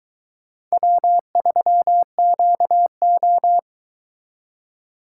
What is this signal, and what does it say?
Morse code, heard as a keyed tone, about 22 wpm, 700 Hz. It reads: W3QO